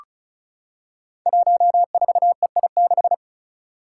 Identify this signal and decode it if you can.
Morse code, heard as a keyed tone, about 34 wpm, 700 Hz. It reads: 14EI6
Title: Five Challenging Letter-Number Combinations 35wpm